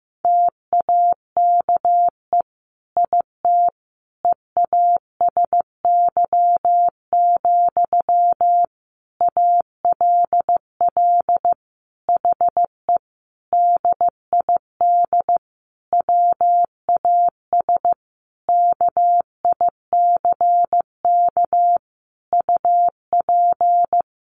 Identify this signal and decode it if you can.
Morse code, heard as a keyed tone, about 15 wpm, 700 Hz. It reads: TAKE IT EASY, ALL HE DID WAS KICK UP
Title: A Sound Of Thunder 15wpm